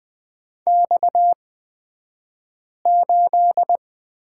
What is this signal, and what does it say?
Morse code, heard as a keyed tone, about 20 wpm, spaced as for 5 wpm, 700 Hz. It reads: X8
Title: 254 Lesson 41d - Period - Two Characters 20wpm